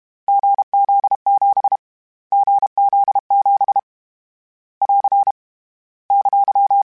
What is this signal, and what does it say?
Morse code, heard as a keyed tone, about 32 wpm, 800 Hz. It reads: GZ7 GZ7 <AR> !
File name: Short Runs / Short Runs-mp3s / GZ GZ7 @30